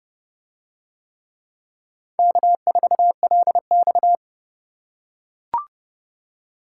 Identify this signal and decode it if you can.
Morse code, heard as a keyed tone, about 29 wpm, 700 Hz. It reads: K4LX